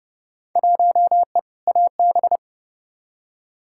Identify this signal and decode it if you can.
Morse code, heard as a keyed tone, about 32 wpm, 700 Hz. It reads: 1E AB
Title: Field Day Exchanges - 30wpm